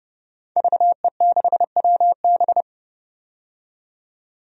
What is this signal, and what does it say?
Morse code, heard as a keyed tone, about 30 wpm, 700 Hz. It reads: VE6WB
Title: DXCC-100 Call Signs - 5 Characters - 30wpm